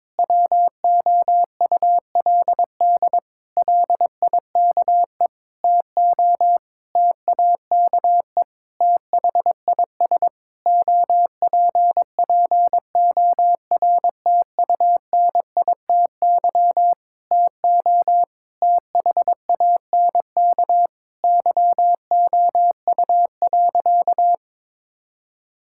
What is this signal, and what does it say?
Morse code, heard as a keyed tone, about 22 wpm, 700 Hz. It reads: WOULD LIKE TO TAKE THIS OPPORTUNITY TO THANK YOU.
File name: practice / 22wpm / Sentences with 8-Word Phrases - 22wpm